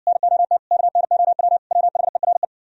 Morse code, heard as a keyed tone, about 59 wpm, 700 Hz. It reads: NOT KNOW WHERE